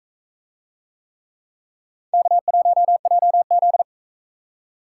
Morse code, heard as a keyed tone, about 44 wpm, 700 Hz. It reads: K1JZ